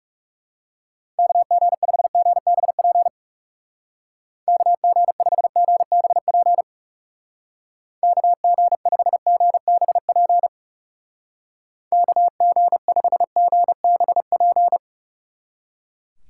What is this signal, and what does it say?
Morse code, heard as a keyed tone, about 37 wpm, 700 Hz. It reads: KG5GBP KG5GBP KG5GBP KG5GBP
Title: Speed Racing - USA 2x3 Call Signs 30wpm